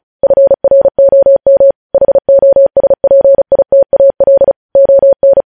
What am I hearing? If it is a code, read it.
Morse code, heard as a keyed tone, about 36 wpm, 550 Hz. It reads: FROM HOSPITAL ON
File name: podcast / News headlines-1x-35x35